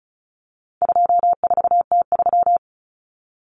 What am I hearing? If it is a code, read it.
Morse code, heard as a keyed tone, about 34 wpm, 700 Hz. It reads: 24T3
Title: Common Cut Numbers - 4-Digit 35wpm